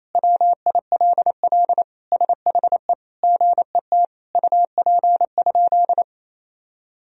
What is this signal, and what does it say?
Morse code, heard as a keyed tone, about 28 wpm, 700 Hz. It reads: WILL SHE GET UP?
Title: Sentences from Top 1000 Words - Review Part 1 - 28wpm